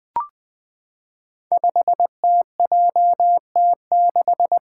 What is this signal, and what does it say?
Morse code, heard as a keyed tone, about 20 wpm, 700 Hz. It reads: 5TJT6
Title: Five Challenging Letter-Number Combinations 20wpm